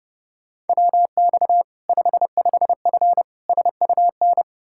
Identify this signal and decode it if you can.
Morse code, heard as a keyed tone, about 30 wpm, 700 Hz. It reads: WX 55F SUN
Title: Practice QSO - Exchange 2 - Info - 30wpm